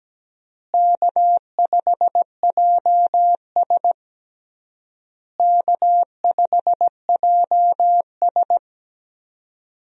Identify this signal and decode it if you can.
Morse code, heard as a keyed tone, about 17 wpm, 700 Hz. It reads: K5JS K5JS
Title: USA Call Signs - 4 Characters - Repeated 3x - 17wpm